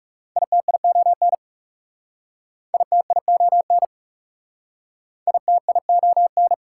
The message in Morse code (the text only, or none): ITION ITION ITION